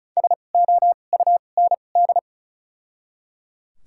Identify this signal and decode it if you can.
Morse code, heard as a keyed tone, about 34 wpm, spaced as for 25 wpm, 700 Hz. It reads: SOUND
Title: 2x Character Spacing - Top 500 Words - 35wpm